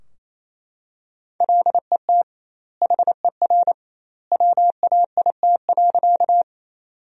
Morse code, heard as a keyed tone, about 28 wpm, 700 Hz. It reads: LET HER WAIT.
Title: 2x Word-Spacing - Sentences from Top 500 Words - 28wpm